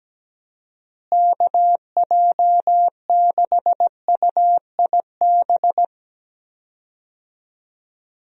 Morse code, heard as a keyed tone, about 17 wpm, 700 Hz. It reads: KJ6UIB
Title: USA Call Signs - 6 Characters - Repeated 3x - 17wpm